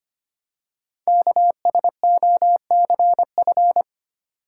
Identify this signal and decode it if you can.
Morse code, heard as a keyed tone, about 25 wpm, 700 Hz. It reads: KSOCF
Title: Code Groups - 5 Letter:Numbers - 25wpm